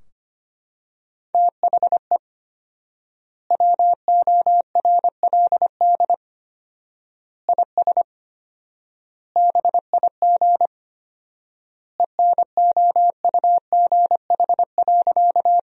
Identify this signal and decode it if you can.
Morse code, heard as a keyed tone, about 25 wpm, 700 Hz. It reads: THE WORLD IS BIG ENOUGH.
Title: 4x Word-Spacing - Sentences from Top 400 Words - 25wpm